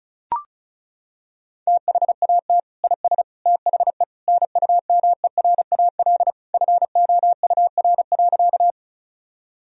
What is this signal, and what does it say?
Morse code, heard as a keyed tone, about 36 wpm, 700 Hz. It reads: THAT IS THE NUMERAL FOUR.
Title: Sentences from Top 1000 Words - Review Part 2 - 35wpm